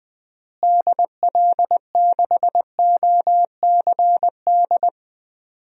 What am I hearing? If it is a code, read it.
Morse code, heard as a keyed tone, about 20 wpm, 700 Hz. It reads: DL6OCD
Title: DXCC-100 Call Signs - 6 Characters - Repeated 3x - 20wpm